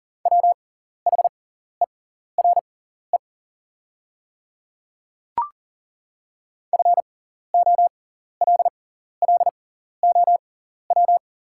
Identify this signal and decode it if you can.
Morse code, heard as a keyed tone, about 40 wpm, spaced as for 14 wpm, 700 Hz. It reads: WHERE FOLLOW
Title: 6x Character Spacing - Top 300 Words - 40wpm